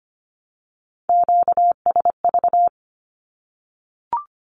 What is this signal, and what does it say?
Morse code, heard as a keyed tone, about 25 wpm, 700 Hz. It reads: QSV
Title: Q-Codes Repeated 3x 25wpm